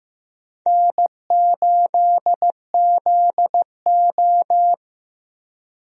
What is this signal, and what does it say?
Morse code, heard as a keyed tone, about 15 wpm, 700 Hz. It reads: N8ZO
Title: USA Call Signs - 4 Characters - Repeated 2x - 15wpm